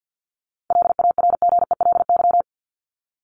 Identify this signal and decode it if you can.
Morse code, heard as a keyed tone, about 50 wpm, 700 Hz. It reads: LARGELY
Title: LY Suffix - Top 100 Words with Base Words 50wpm